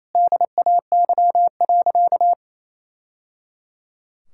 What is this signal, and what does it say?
Morse code, heard as a keyed tone, about 28 wpm, 700 Hz. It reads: DAY.
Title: Sentences from Top 1000 Words - Review Part 5 - 28wpm